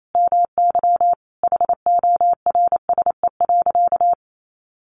Morse code, heard as a keyed tone, about 28 wpm, 700 Hz. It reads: MY HORSE.
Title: Sentences from Top 500 Words 28wpm